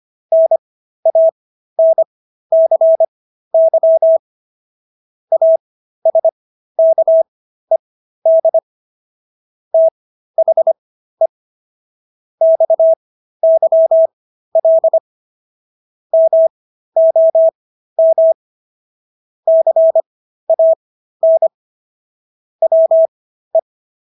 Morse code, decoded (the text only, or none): NANCY ASKED THE XYL MOM CAN WE